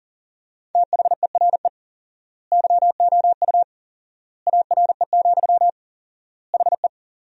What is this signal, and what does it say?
Morse code, heard as a keyed tone, about 40 wpm, 700 Hz. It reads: THERE YOU ARE, HE
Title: 4x Word-Spacing - Sentences from Top 100 Words - 40wpm